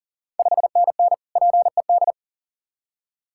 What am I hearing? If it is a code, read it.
Morse code, heard as a keyed tone, about 39 wpm, 700 Hz. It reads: 5NN PED
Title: Ontario QSO Party Exchange 40wpm